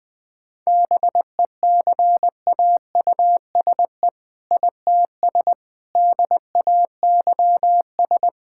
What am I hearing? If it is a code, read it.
Morse code, heard as a keyed tone, about 21 wpm, 700 Hz. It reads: BECAUSE ITS DAYS
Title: All-Age Appropriate Jokes - 20wpm